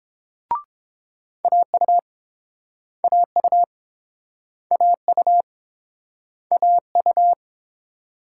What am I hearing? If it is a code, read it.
Morse code, heard as a keyed tone, about 27 wpm, 700 Hz. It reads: AU AU AU AU
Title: Speed-Racing Multi-letter Phonemes - 22wpm